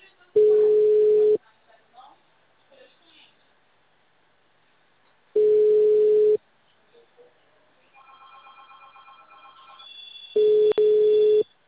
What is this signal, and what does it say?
Ringback tone (Europe, 425 Hz), with speech.
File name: out- -2045- - - .16964